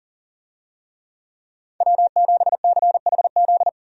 Morse code, heard as a keyed tone, about 40 wpm, 700 Hz. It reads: W7CHZ
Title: USA Call Signs - 5 Characters - 40wpm